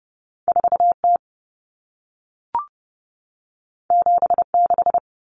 Morse code, heard as a keyed tone, about 30 wpm, 700 Hz. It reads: 4T 76
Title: Common Cut Numbers - 2-Digit 30wpm